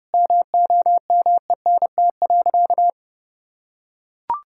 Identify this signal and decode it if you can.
Morse code, heard as a keyed tone, about 30 wpm, 700 Hz. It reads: MOMENT.
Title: Sentences from Top 1000 Words - Review Part 6 - 30wpm